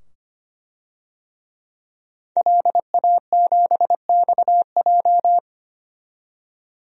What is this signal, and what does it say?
Morse code, heard as a keyed tone, about 25 wpm, 700 Hz. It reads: LA7XJ
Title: DXCC-100 Call Signs - 5 Characters - 25wpm